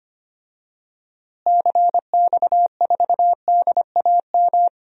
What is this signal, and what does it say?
Morse code, heard as a keyed tone, about 25 wpm, 700 Hz. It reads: CX4DAM
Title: DXCC-100 Call Signs - 6 Characters - Repeated 2x - 25wpm